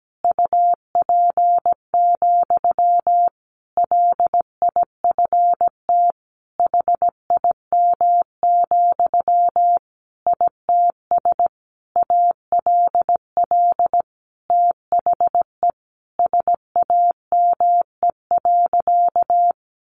Morse code, heard as a keyed tone, about 17 wpm, 700 Hz. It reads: UP, LIFT HIM, ITS ALL THE SAME.
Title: Sentences from Top 1000 Words 17wpm